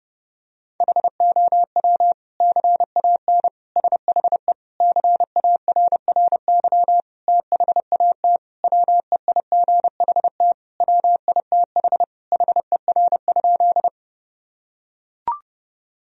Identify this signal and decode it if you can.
Morse code, heard as a keyed tone, about 30 wpm, 700 Hz. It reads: HOW CAN SHE CARRY THAT WEIGHT WITH HER?
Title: Sentences from Top 1000 Words - Review Part 3 - 30wpm